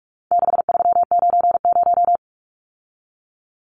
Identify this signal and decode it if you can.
Morse code, heard as a keyed tone, about 44 wpm, 700 Hz. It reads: 6390
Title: Numbers - 4-Digit 45wpm